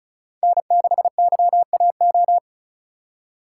Morse code, heard as a keyed tone, about 34 wpm, 700 Hz. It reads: N6YAO
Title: USA Call Signs - 3x Character Spacing - 35wpm